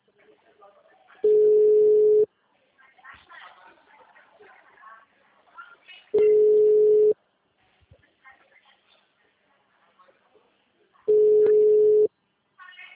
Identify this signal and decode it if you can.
Ringback tone (Europe, 425 Hz), with speech.
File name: out- -2006- - - .82148